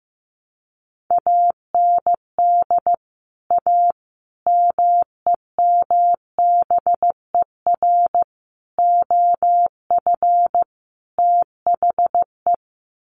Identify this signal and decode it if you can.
Morse code, heard as a keyed tone, about 15 wpm, 700 Hz. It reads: AND A MEMBER OF THE
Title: Sets of 5 Words 15wpm